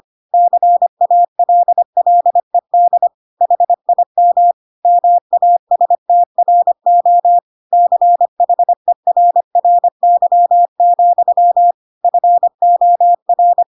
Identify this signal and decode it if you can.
Morse code, heard as a keyed tone, about 25 wpm, 700 Hz. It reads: CALLED HIM MASTRO CHERRY, FOR
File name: Pinocchio / Pinocchio ch1 25 25 ik1hge